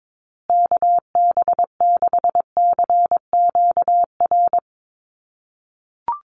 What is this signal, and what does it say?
Morse code, heard as a keyed tone, about 22 wpm, 700 Hz. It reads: KB6CQR
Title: USA Call Signs Repeated 3x 22wpm